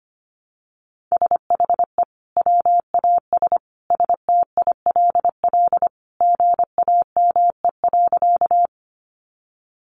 Morse code, heard as a keyed tone, about 25 wpm, 700 Hz. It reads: SHE WAS STILL GAME.
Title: Sentences from Top 500 Words 25wpm